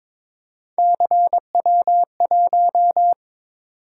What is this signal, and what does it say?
Morse code, heard as a keyed tone, about 23 wpm, 700 Hz. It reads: CW1
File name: practice / 22wpm / Three Letter-Number Combinations - 22wpm